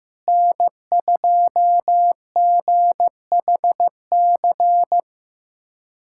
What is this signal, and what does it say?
Morse code, heard as a keyed tone, about 16 wpm, 700 Hz. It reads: N2GHC
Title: USA Callsign DE Callsign - 5 Characters - 15wpm